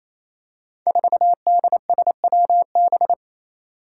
Morse code, heard as a keyed tone, about 28 wpm, 700 Hz. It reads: 4DSWB
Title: Five Challenging Letter-Number Combinations 28wpm